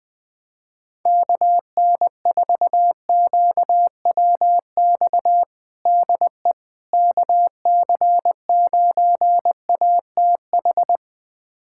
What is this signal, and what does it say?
Morse code, heard as a keyed tone, about 20 wpm, 700 Hz. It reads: KN4QWX DE KC9ATH
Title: USA Callsign DE Callsign - 6 Characters - 20wpm